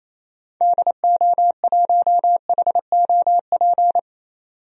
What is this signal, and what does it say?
Morse code, heard as a keyed tone, about 27 wpm, 700 Hz. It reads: DO1HOP
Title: DXCC-100 Call Signs - 6 Characters - 28wpm